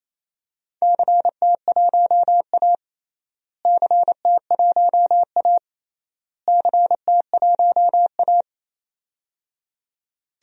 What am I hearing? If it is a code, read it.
Morse code, heard as a keyed tone, about 27 wpm, 700 Hz. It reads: CT1A CT1A CT1A
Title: DXCC-100 Call Signs - 4 Characters - Repeated 3x - 28wpm